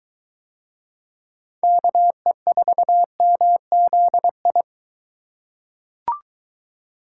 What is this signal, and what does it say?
Morse code, heard as a keyed tone, about 21 wpm, 700 Hz. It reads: KE4MZI